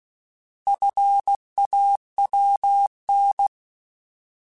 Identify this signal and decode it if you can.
Morse code, heard as a keyed tone, about 16 wpm, 800 Hz. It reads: FAWN
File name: Alphabetical / Alphabetical mp3s / Fs @15